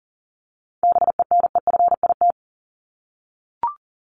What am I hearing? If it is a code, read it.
Morse code, heard as a keyed tone, about 40 wpm, 700 Hz. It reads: BENEFIT
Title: Top 751-1000 Words 40wpm